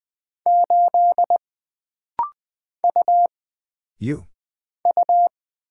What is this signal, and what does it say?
Morse code, heard as a keyed tone, about 20 wpm, 700 Hz. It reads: 8 U U
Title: Single Character (Letter:Number:Punctuation:Prosign) - Rapid-Fire - 20wpm